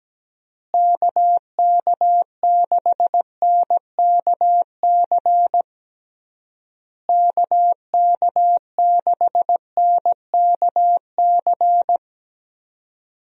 Morse code, heard as a keyed tone, about 17 wpm, 700 Hz. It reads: KK6NKC KK6NKC
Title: USA Call Signs - 6 Characters - Repeated 3x - 17wpm